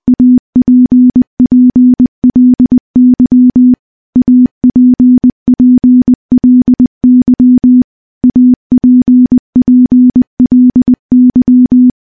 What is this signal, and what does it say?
Morse code, heard as a keyed tone, about 21 wpm, 250 Hz. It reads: APPLY APPLY APPLY
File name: morse-training-20WPM